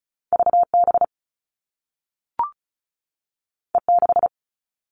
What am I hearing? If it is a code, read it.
Morse code, heard as a keyed tone, about 36 wpm, 690 Hz. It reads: VB E6
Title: All Cut Numbers - 2-Digit 35wpm